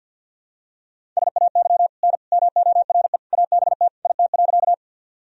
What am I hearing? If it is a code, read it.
Morse code, heard as a keyed tone, about 52 wpm, 700 Hz. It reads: SAY N MORE ABT IT.
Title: Sentences from Top 200 Words and Abbreviations 50wpm